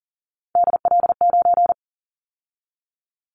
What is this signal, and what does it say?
Morse code, heard as a keyed tone, about 39 wpm, 700 Hz. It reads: DL9
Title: Callsign Prefixes for Top 50 DXCC - 40wpm